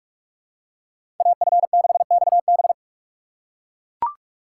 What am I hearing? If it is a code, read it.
Morse code, heard as a keyed tone, about 42 wpm, 700 Hz. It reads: AF6XB
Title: USA Call Signs - 5 Characters - 4x Character Spacing - 45wpm